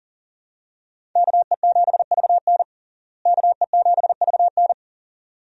Morse code, heard as a keyed tone, about 40 wpm, 700 Hz. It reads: KE7VN KE7VN
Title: USA Call Signs - 5 Characters - Repeated 2x - 40wpm